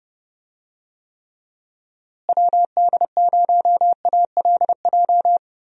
Morse code, heard as a keyed tone, about 30 wpm, 700 Hz. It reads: WD0ALJ
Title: USA Call Signs - 6 Characters - Repeated 3x - 30wpm